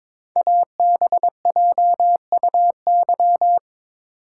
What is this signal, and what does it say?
Morse code, heard as a keyed tone, about 22 wpm, 700 Hz. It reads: ABJUY
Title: Code Groups - 5 Letters - 22wpm